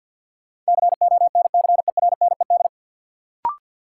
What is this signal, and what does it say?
Morse code, heard as a keyed tone, about 53 wpm, 700 Hz. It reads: CONCERNED
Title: Sentences from Top 1000 Words - Review Part 6 - New Word Drill - 50wpm